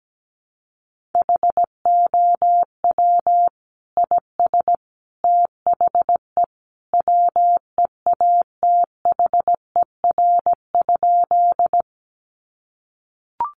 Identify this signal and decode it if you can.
Morse code, heard as a keyed tone, about 17 wpm, 700 Hz. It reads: HOW IS THE WEATHER?
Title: Sentences from Top 800 Words 17wpm